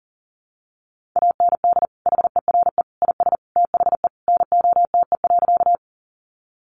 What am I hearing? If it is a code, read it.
Morse code, heard as a keyed tone, about 40 wpm, 700 Hz. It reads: AND HERE IS THE NOTE.
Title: Sentences from Top 500 Words 40wpm